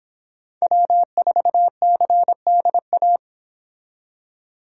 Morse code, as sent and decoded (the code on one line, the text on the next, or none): .-- ....- -.-. -.. .-
W4CDA